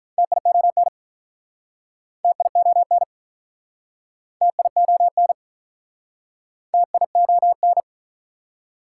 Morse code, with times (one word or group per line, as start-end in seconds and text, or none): 0.2-0.9 TION
2.2-3.0 TION
4.4-5.3 TION
6.7-7.8 TION